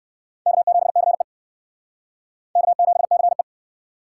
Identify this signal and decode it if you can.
Morse code, heard as a keyed tone, about 68 wpm, 700 Hz. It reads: K7CE K7CE